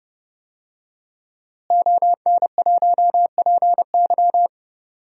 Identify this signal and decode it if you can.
Morse code, heard as a keyed tone, about 30 wpm, 700 Hz. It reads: ON1PY
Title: DXCC-100 Call Signs - 5 Characters - Repeated 2x - 30wpm